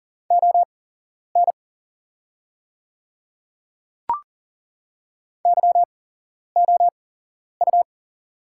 Morse code, heard as a keyed tone, about 40 wpm, spaced as for 11 wpm, 690 Hz. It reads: ON YOU
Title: 8x Character Spacing - Top 200 Words - 40wpm